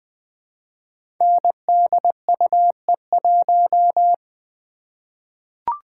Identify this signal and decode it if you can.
Morse code, heard as a keyed tone, about 20 wpm, 700 Hz. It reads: NDUE1
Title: Code Groups - 5 Letter:Numbers - 20wpm